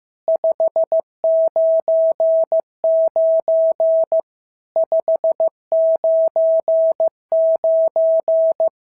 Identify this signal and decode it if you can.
Morse code, heard as a keyed tone, about 15 wpm, 650 Hz. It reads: 599 599
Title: SOTA QSO Practice 15wpm